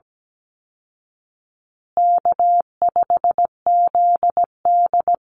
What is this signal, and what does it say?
Morse code, heard as a keyed tone, about 17 wpm, 700 Hz. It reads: K5ZD
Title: Easy Call Signs Repeated 2x - 17wpm